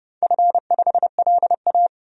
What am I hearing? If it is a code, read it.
Morse code, heard as a keyed tone, about 30 wpm, 700 Hz. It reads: F5LA
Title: DXCC-100 Call Signs - 4 Characters - Repeated 3x - 30wpm